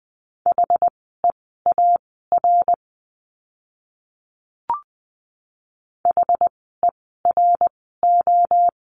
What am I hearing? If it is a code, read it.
Morse code, heard as a keyed tone, about 20 wpm, spaced as for 15 wpm, 700 Hz. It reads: HEAR HERO
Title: 068 Lesson 12f - Letter d - Words 20wpm